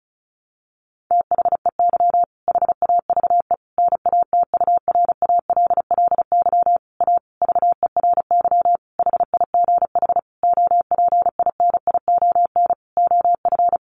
Morse code, heard as a keyed tone, about 34 wpm, 700 Hz. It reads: THEY HAVE NATURALLY A VERY HIGH OPINION OF
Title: Sentences from Top 1300 Words 35wpm